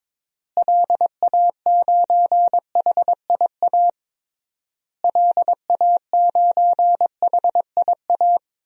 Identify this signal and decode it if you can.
Morse code, heard as a keyed tone, about 22 wpm, 700 Hz. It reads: LA9HIA LA9HIA
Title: DXCC-100 Call Signs - 6 Characters - Repeated 3x - 22wpm